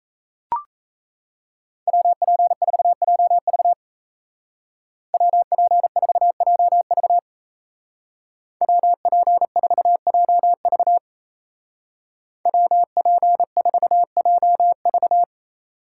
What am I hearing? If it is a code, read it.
Morse code, heard as a keyed tone, about 35 wpm, 700 Hz. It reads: WP4JV WP4JV WP4JV WP4JV
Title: Speed Racing - USA Call Signs 28wpm